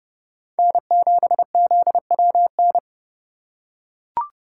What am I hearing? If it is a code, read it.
Morse code, heard as a keyed tone, about 29 wpm, 700 Hz. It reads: N7ZWN